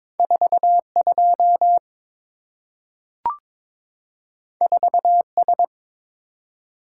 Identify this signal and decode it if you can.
Morse code, heard as a keyed tone, about 22 wpm, 700 Hz. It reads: 42 4S
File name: practice / 22wpm / Two Letter-Number Combinations - 22wpm